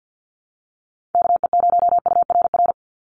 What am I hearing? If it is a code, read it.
Morse code, heard as a keyed tone, about 50 wpm, 700 Hz. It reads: KE0URR
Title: USA Call Signs 50wpm